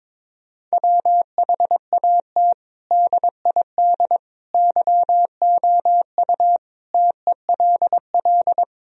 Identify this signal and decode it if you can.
Morse code, heard as a keyed tone, about 22 wpm, 700 Hz. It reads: WHAT DID YOU TELL
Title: Sentences from Top 1000 Words - Review Part 4 - 22wpm